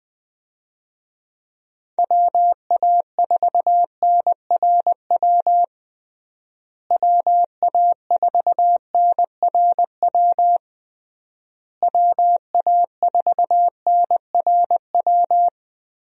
Morse code, heard as a keyed tone, about 20 wpm, 700 Hz. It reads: WA4NRW WA4NRW WA4NRW
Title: USA Call Signs - 6 Characters - Repeated 3x - 20wpm